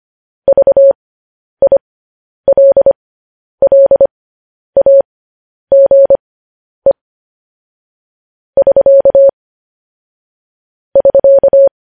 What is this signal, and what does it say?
Morse code, heard as a keyed tone, about 25 wpm, spaced as for 10 wpm, 550 Hz. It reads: VILLAGE <SK> <SK>